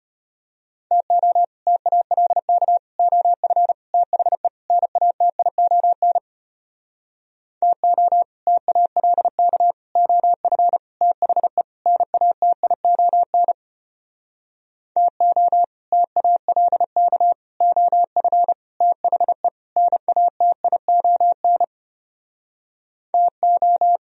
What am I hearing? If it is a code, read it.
Morse code, heard as a keyed tone, about 33 wpm, 700 Hz. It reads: TO TALK OF THE NATION TO TALK OF THE NATION TO TALK OF THE NATION TO